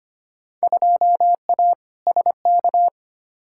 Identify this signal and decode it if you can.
Morse code, heard as a keyed tone, about 25 wpm, 700 Hz. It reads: 2A SK
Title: Field Day Exchanges Encore - 25wpm